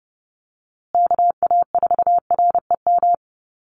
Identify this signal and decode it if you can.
Morse code, heard as a keyed tone, about 30 wpm, 700 Hz. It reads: KA4REM
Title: USA Call Signs Repeated 2x 30wpm